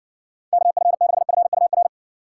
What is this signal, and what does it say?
Morse code, heard as a keyed tone, about 60 wpm, 700 Hz. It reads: KF6FUR